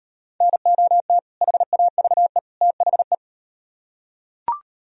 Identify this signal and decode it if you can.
Morse code, heard as a keyed tone, about 38 wpm, 700 Hz. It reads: NOT HAVE THE